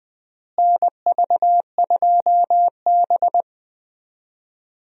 Morse code, heard as a keyed tone, about 20 wpm, 700 Hz. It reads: NV2B
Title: USA Call Signs - 4 Characters - Repeated 2x - 20wpm